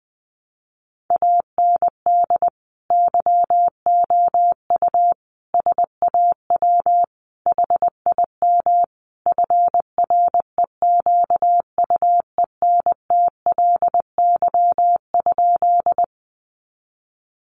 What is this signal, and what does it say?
Morse code, heard as a keyed tone, about 20 wpm, 700 Hz. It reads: AND YOU SAW HIM FREQUENTLY?
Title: Sentences from Top 1300 Words 20wpm